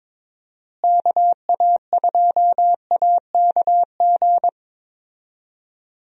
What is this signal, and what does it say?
Morse code, heard as a keyed tone, about 22 wpm, 700 Hz. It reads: KA2AKG
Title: USA Call Signs - 6 Characters - Repeated 3x - 22wpm